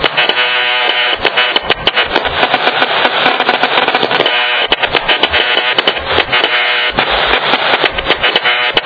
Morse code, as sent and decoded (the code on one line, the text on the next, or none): none